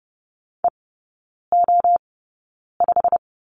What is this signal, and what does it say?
Morse code, heard as a keyed tone, about 30 wpm, spaced as for 9 wpm, 700 Hz. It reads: EO5
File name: course / 30:10wpm / 087 Lesson 15e - Number 5 - Three Characters 30wpm